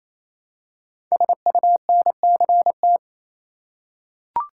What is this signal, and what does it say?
Morse code, heard as a keyed tone, about 27 wpm, 700 Hz. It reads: SUNCT
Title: Code Groups - 5 Letter:Numbers - 28wpm